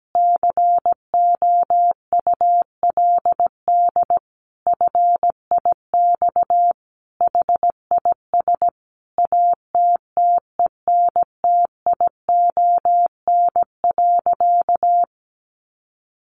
Morse code, heard as a keyed tone, about 17 wpm, 700 Hz. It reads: COULD FIX HIS ATTENTION.
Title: Sentences from Top 1400 Words 17wpm